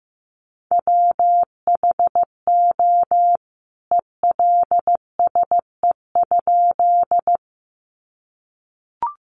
Morse code, heard as a keyed tone, about 15 wpm, 700 Hz. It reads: WHO ELSE?
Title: A Sound Of Thunder 15wpm